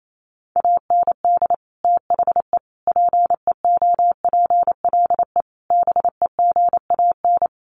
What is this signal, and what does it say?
Morse code, heard as a keyed tone, about 28 wpm, 700 Hz. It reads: AND THE PEOPLE BEGAN
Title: Sentences from Top 1500 Words 28wpm